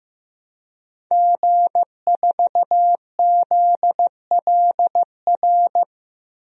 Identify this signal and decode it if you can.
Morse code, heard as a keyed tone, about 15 wpm, 700 Hz. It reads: G4ZLR
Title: DXCC-100 Call Signs - 5 Characters - 15wpm